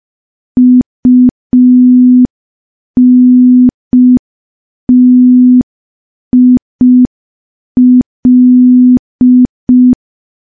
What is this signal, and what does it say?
Morse code, heard as a keyed tone, about 5 wpm, 250 Hz. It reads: UNTIL
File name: morse / morse-training-05WPM